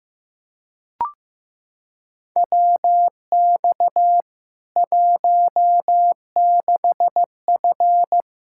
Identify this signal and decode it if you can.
Morse code, heard as a keyed tone, about 15 wpm, 700 Hz. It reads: WX 16F
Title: Practice QSO - Exchange 2 - Info - 15wpm